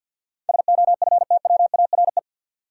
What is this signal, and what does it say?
Morse code, heard as a keyed tone, about 53 wpm, 700 Hz. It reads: SOFTWARE
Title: Sentences from Top 1000 Words - Review Part 3 - New Word Drill - 50wpm